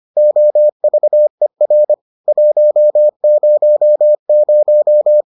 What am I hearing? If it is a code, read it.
Morse code, heard as a keyed tone, about 25 wpm, 600 Hz. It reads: OVER 100